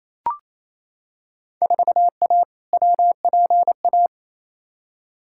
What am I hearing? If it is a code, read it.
Morse code, heard as a keyed tone, about 28 wpm, 700 Hz. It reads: 4A WPA
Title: Field Day Exchanges Encore - 28wpm